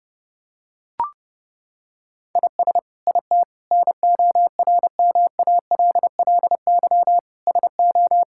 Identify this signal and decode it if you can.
Morse code, heard as a keyed tone, about 30 wpm, 700 Hz. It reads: IS IT NORMALLY SO
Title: Sentences from Top 1000 Words - Review Part 6 - 30wpm